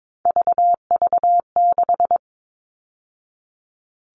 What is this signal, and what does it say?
Morse code, heard as a keyed tone, about 22 wpm, 700 Hz. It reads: VV6
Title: All Cut Numbers - 3-Digit 22wpm